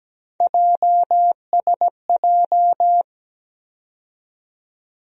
Morse code, heard as a keyed tone, about 17 wpm, 700 Hz. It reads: JSJ
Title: USA Call Sign Postfixes - 3 Letters 17wpm